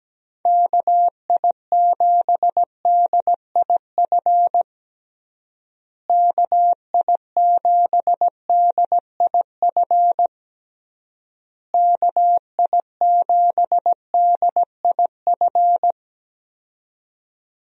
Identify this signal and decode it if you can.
Morse code, heard as a keyed tone, about 17 wpm, 700 Hz. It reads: KI7DIF KI7DIF KI7DIF
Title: USA Call Signs - 6 Characters - Repeated 3x - 17wpm